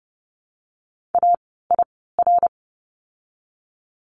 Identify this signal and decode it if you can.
Morse code, heard as a keyed tone, about 30 wpm, 700 Hz. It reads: A I R
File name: Lesson 16f - Letter c - Words 30wpm